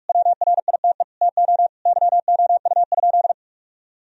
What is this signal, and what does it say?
Morse code, heard as a keyed tone, about 46 wpm, 700 Hz. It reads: WRITE TO YOU?
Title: Sentences from Top 1000 Words - Review Part 3 - 45wpm